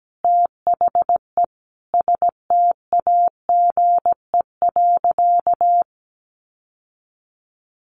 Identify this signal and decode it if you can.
Morse code, heard as a keyed tone, about 17 wpm, 700 Hz. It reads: THE STAGE.
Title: Sentences from Top 1200 Words 17wpm